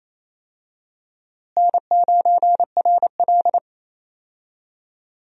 Morse code, heard as a keyed tone, about 28 wpm, 700 Hz. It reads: N9RL
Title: USA Call Signs - 4 Characters - Repeated 2x - 28wpm